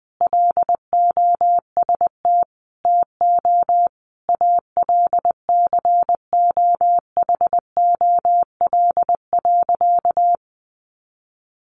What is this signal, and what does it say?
Morse code, heard as a keyed tone, about 19 wpm, 700 Hz. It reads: LOST TO ALCOHOL.
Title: Sentences from Top 1400 Words 20wpm